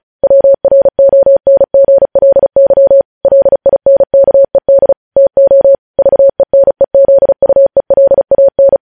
Morse code, heard as a keyed tone, about 36 wpm, 550 Hz. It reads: WRONGLY LINKED TO VENEZUELAN
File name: podcast / News headlines-1x-35x35